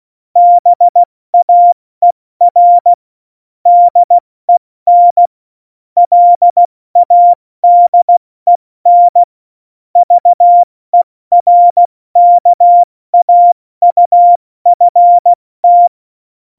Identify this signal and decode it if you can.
Morse code, heard as a keyed tone, about 16 wpm, 700 Hz. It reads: BAER DEN LADEN VERKAUFT
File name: Morsekurs / Witze / Witz 27. November 2018, 70 F80